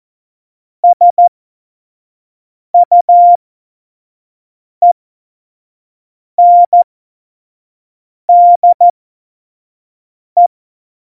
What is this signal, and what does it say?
Morse code, heard as a keyed tone, about 14 wpm, spaced as for 5 wpm, 700 Hz. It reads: SUENDE
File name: Morsekurs / Witze / Witz 29. Dezember 2020, 25 F70